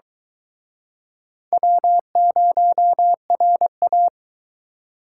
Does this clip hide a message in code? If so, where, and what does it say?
Yes: Morse code from 1.5 to 4.1 s, heard as a keyed tone, about 23 wpm, 700 Hz. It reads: W0RA